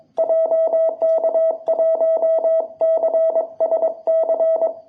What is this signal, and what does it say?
Morse code, heard as a keyed tone, about 22 wpm, 660 Hz. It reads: JK1CSC
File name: jh8cbh / 2021cw